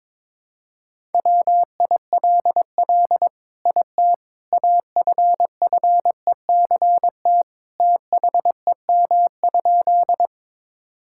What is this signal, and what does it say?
Morse code, heard as a keyed tone, about 21 wpm, 700 Hz. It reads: WILL IT AFFECT THEM?
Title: Sentences from Top 1000 Words - Review Part 6 - 22wpm